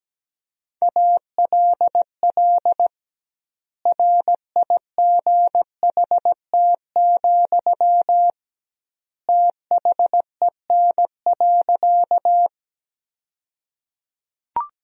Morse code, heard as a keyed tone, about 17 wpm, 700 Hz. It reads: ALL RIGHT, THEN.
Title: 2x Word-Spacing - Sentences from Top 200 Words - 17wpm